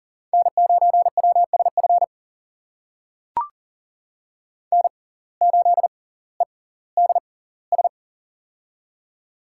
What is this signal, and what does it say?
Morse code, heard as a keyed tone, about 40 wpm, spaced as for 21 wpm, 690 Hz. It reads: N9WSF N8EDS
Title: USA Call Signs - 5 Characters - 6x Character Spacing - 40wpm